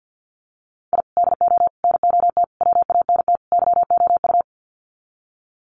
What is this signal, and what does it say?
Morse code, heard as a keyed tone, about 50 wpm, 700 Hz. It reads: I DO NOT WANT YOU
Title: Sets of 5 Words 50wpm